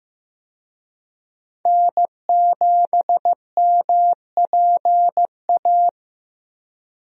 Morse code, heard as a keyed tone, about 15 wpm, 700 Hz. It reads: N7MPA
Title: USA Call Signs - 5 Characters - Repeated 3x - 15wpm